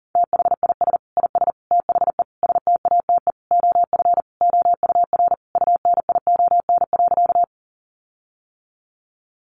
Morse code, heard as a keyed tone, about 40 wpm, 700 Hz. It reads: THIS IS THE STATE OF OUR UNION.
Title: Sentences from Top 1400 Words 40wpm